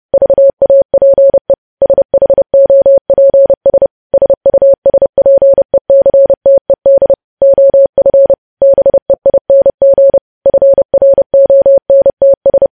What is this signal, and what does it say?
Morse code, heard as a keyed tone, about 30 wpm, 550 Hz. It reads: VAPE SHOPS SUSPECTED OF BEING FRONTS
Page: news headlines in morse code at 30 wpm